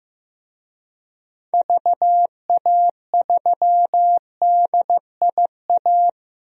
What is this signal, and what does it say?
Morse code, heard as a keyed tone, about 15 wpm, 700 Hz. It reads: VA3DIA
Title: DXCC-100 Call Signs - 6 Characters - Repeated 2x - 15wpm